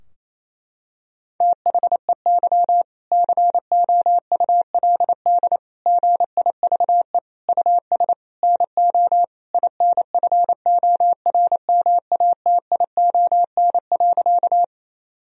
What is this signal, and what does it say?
Morse code, heard as a keyed tone, about 28 wpm, 700 Hz. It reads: THEY COULD GIVE US NO INFORMATION.
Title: Sentences from Top 1000 Words - Review Part 3 - 28wpm